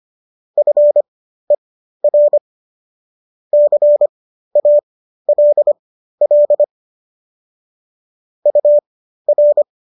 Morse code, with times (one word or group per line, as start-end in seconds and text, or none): 0.6-2.4 FER
3.5-6.6 CALL
8.5-9.6 UR